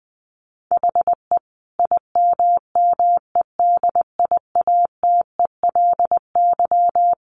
Morse code, heard as a keyed tone, about 19 wpm, 700 Hz. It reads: HE IMMEDIATELY
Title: Sentences from Top 1100 Words 20wpm